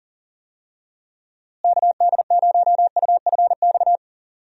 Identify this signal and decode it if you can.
Morse code, heard as a keyed tone, about 40 wpm, 700 Hz. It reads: KD0UFX
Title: USA Call Signs - 6 Characters - Repeated 2x - 40wpm